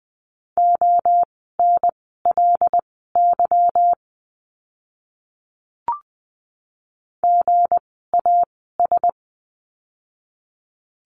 Morse code, heard as a keyed tone, about 20 wpm, spaced as for 14 wpm, 700 Hz. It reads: ONLY GAS
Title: 160 Lesson 26f - Letter g - Words 20wpm